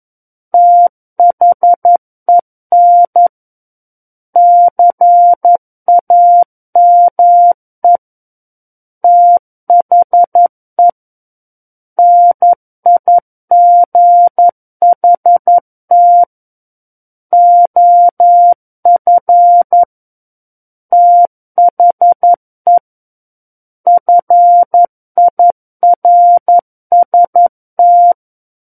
Morse code, heard as a keyed tone, about 11 wpm, 700 Hz. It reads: THEN CAME THE NIGHT OF THE FIRST
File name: 11 WPM